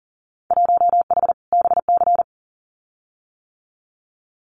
Morse code, heard as a keyed tone, about 38 wpm, 700 Hz. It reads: 1H BC
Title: Winter Field Day Exchanges 40wpm